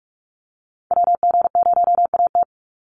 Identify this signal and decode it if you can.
Morse code, heard as a keyed tone, about 44 wpm, 700 Hz. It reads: WG0AT
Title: NA SOTA Activator Callsigns 45wpm